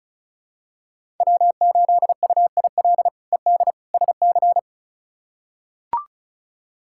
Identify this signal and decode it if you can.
Morse code, heard as a keyed tone, about 36 wpm, 700 Hz. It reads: W8UIL ED SC
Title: K1USN SST QSO Practice - 35wpm